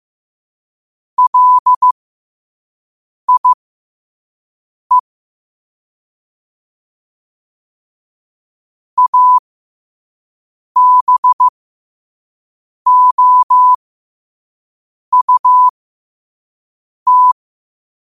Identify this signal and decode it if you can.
Morse code, heard as a keyed tone, about 15 wpm, spaced as for 5 wpm, 1000 Hz. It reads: LIE ABOUT
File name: qotdfiles / 5wpm